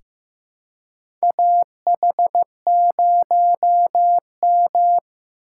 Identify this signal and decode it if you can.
Morse code, heard as a keyed tone, about 15 wpm, 700 Hz. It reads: AH0M
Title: USA Call Signs - 4x Character Spacing - 15wpm